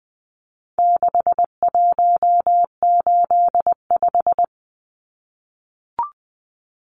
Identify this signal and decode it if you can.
Morse code, heard as a keyed tone, about 20 wpm, 700 Hz. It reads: 6185
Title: Numbers - 4-Digit 20wpm